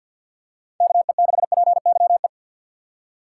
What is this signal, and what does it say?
Morse code, heard as a keyed tone, about 50 wpm, 700 Hz. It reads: KE6PYE
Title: USA Call Signs - 6 Characters - 8x Character Spacing - 50wpm